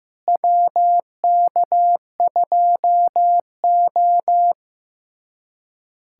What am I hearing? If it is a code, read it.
Morse code, heard as a keyed tone, about 15 wpm, 700 Hz. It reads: WK2O
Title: USA Callsign DE Callsign - 4 Characters - 15wpm